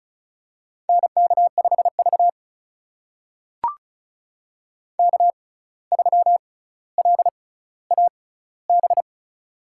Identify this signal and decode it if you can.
Morse code, heard as a keyed tone, about 36 wpm, spaced as for 17 wpm, 700 Hz. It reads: NK5V K3LAB
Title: USA Call Signs - 6x Character Spacing - 35wpm